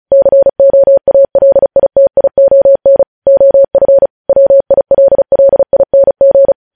Morse code, heard as a keyed tone, about 36 wpm, 550 Hz. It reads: COALITION OF WILLING
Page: News Headlines in Morse Code at 35 WPM